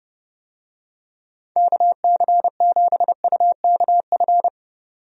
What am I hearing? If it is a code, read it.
Morse code, heard as a keyed tone, about 32 wpm, 700 Hz. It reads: KC7UKF
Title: USA Call Signs - 6 Characters - 30wpm